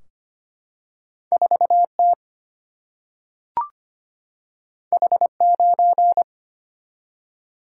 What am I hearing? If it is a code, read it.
Morse code, heard as a keyed tone, about 25 wpm, 700 Hz. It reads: 4T H9
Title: Single-Letter Single-Number 25wpm